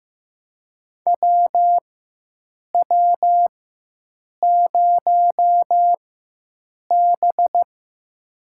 Morse code, heard as a keyed tone, about 15 wpm, spaced as for 7 wpm, 700 Hz. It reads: WW0B
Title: USA Call Signs - 4 Characters - 4x Character Spacing - 15wpm